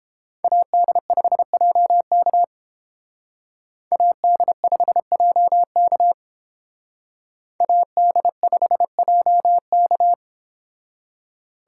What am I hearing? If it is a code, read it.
Morse code, heard as a keyed tone, about 29 wpm, 700 Hz. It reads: AD5JK AD5JK AD5JK